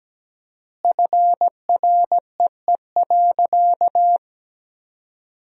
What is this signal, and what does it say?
Morse code, heard as a keyed tone, about 17 wpm, 700 Hz. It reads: FREE.
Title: 4x Word-Spacing - Sentences from Top 500 Words - 17wpm